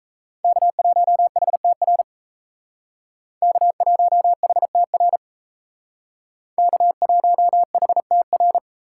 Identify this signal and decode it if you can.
Morse code, heard as a keyed tone, about 35 wpm, 700 Hz. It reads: K1HTR K1HTR K1HTR